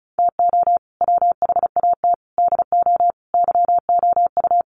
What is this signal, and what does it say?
Morse code, heard as a keyed tone, about 34 wpm, 700 Hz. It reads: TO WHAT DO YOU
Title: Sentences from Top 1000 Words 35wpm